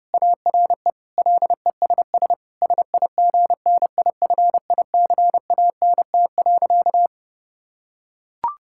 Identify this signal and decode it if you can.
Morse code, heard as a keyed tone, about 32 wpm, 700 Hz. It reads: ARE LESS SIGNIFICANT.
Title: Sentences from Top 1000 Words - Review Part 6 - 30wpm